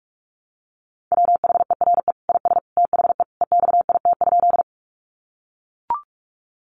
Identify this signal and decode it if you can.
Morse code, heard as a keyed tone, about 44 wpm, 700 Hz. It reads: WHERE IS THE EXIT?
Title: Sentences from Top 1100 Words 45wpm